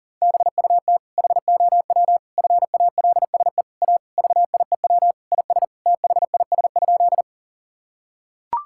Morse code, heard as a keyed tone, about 40 wpm, 700 Hz. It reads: BUT HOW FALSE A VIEW IS THIS?
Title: Sentences from Top 1600 Words 40wpm